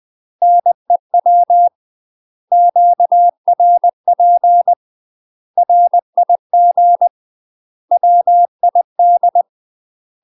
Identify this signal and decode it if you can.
Morse code, heard as a keyed tone, about 20 wpm, 700 Hz. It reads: NEW QRP RIG WID
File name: UK MASTER QSOs TRAINING / 20WPM / UK MASTER QSOs 0038